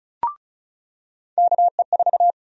Morse code, heard as a keyed tone, about 36 wpm, 700 Hz. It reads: KE4
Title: USA Call Sign Prefixes - 35wpm